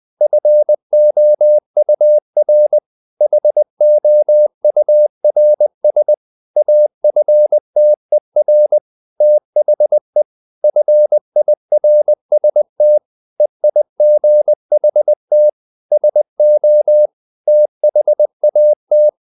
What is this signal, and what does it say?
Morse code, heard as a keyed tone, about 20 wpm, 600 Hz. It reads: FOUR HOURS AFTER THE FIRST EIGHT SO THAT
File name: the old man and the sea part 2 20wpm